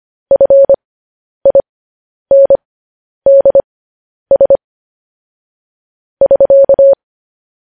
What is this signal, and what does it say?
Morse code, heard as a keyed tone, about 25 wpm, spaced as for 10 wpm, 550 Hz. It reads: FINDS <SK>
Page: news headlines in morse code at 10 wpm